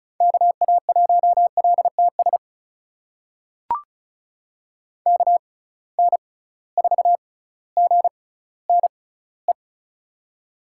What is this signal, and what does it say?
Morse code, heard as a keyed tone, about 36 wpm, spaced as for 18 wpm, 700 Hz. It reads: KA1LTS KN4GNE
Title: USA Call Signs - 6x Character Spacing - 35wpm